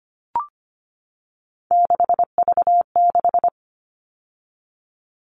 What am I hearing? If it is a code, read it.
Morse code, heard as a keyed tone, about 25 wpm, 700 Hz. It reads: 6V6
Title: All Cut Numbers - 3-Digit 25wpm